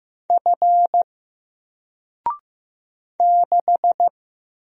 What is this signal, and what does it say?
Morse code, heard as a keyed tone, about 15 wpm, 700 Hz. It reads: F 6
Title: Single Letter-Number Alt 15wpm